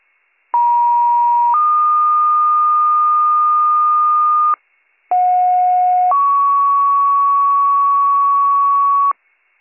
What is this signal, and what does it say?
Two-tone page, heard as a one-second tone, then a three-second tone, about 950 then 1220 Hz, about 730 then 1120 Hz. It is heard from 0.5 to 9.1 s.